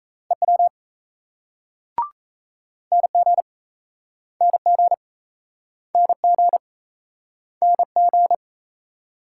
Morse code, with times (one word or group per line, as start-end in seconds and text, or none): 0.3-0.7 EW
2.9-3.4 NG
4.4-4.9 NG
5.9-6.6 NG
7.6-8.3 NG